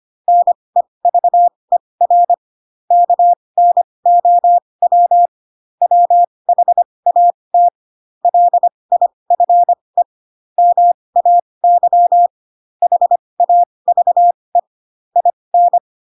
Morse code, heard as a keyed tone, about 25 wpm, spaced as for 20 wpm, 700 Hz. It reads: NEVER KNOW WHAT LIFE MAY HAVE IN
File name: Pinocchio / Pinocchio ch7 15 25 ik1hge